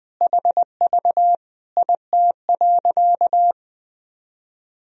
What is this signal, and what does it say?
Morse code, heard as a keyed tone, about 20 wpm, 700 Hz. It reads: HV IT.
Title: Sentences from Top 300 Words and Abbreviations 20wpm